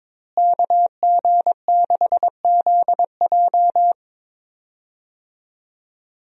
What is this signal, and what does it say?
Morse code, heard as a keyed tone, about 22 wpm, 700 Hz. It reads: KG6ZJ
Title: USA Call Signs - 5 Characters - Repeated 3x - 22wpm